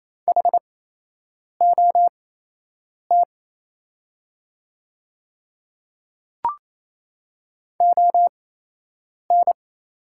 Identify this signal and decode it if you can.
Morse code, heard as a keyed tone, about 28 wpm, spaced as for 8 wpm, 700 Hz. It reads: HOT ON
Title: 8x Character Spacing - Top 100 Words - 28wpm